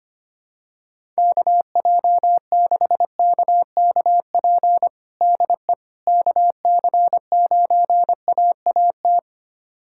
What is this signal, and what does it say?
Morse code, heard as a keyed tone, about 25 wpm, 700 Hz. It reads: KJ6KKP DE KC9AAT
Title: USA Callsign DE Callsign - 6 Characters - 25wpm